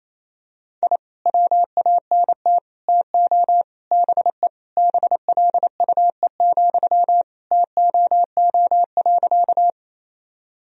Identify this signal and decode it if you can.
Morse code, heard as a keyed tone, about 28 wpm, 700 Hz. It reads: I WANT TO BE BLUE, TOO.
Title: Sentences from Top 1000 Words - Review Part 3 - 28wpm